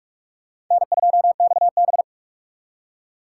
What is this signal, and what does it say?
Morse code, heard as a keyed tone, about 45 wpm, 700 Hz. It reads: N2XB
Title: USA Call Signs - 6x Character Spacing - 45wpm